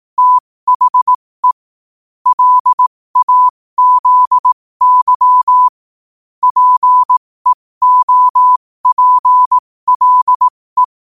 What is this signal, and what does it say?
Morse code, heard as a keyed tone, about 18 wpm, spaced as for 15 wpm, 1000 Hz. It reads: THE LAZY PEOPLE
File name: qotdfiles / 15wpm